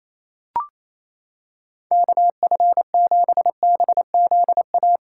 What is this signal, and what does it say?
Morse code, heard as a keyed tone, about 28 wpm, 700 Hz. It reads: KF7BZA
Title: USA Callsign DE Callsign - 6 Characters - 28wpm